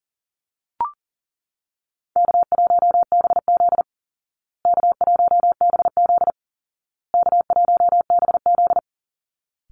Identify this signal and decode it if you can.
Morse code, heard as a keyed tone, about 40 wpm, 700 Hz. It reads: K1BZ K1BZ K1BZ
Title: Easy Call Signs Repeated 3x - 40wpm